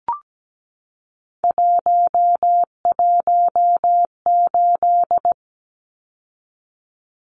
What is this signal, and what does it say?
Morse code, heard as a keyed tone, about 17 wpm, 700 Hz. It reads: 118
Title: Numbers - 3-Digit 17wpm